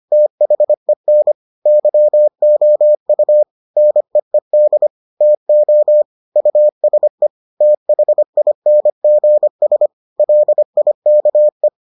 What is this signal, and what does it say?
Morse code, heard as a keyed tone, about 25 wpm, 600 Hz. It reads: THEN YOU NEED TO USE THINGS LIKE